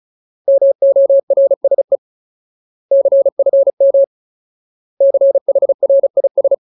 Morse code, heard as a keyed tone, about 34 wpm, 550 Hz. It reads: MORSE CFM CHRIS
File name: Ultimate Word List / 35WPM Ultimate